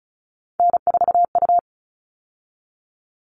Morse code, heard as a keyed tone, about 34 wpm, 700 Hz. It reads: N4U
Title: All Cut Numbers - 3-Digit 35wpm